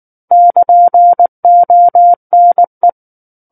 Morse code, heard as a keyed tone, about 19 wpm, 700 Hz. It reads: <KN>ONE
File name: Book One / 10-Chapter 10 - 19 WPM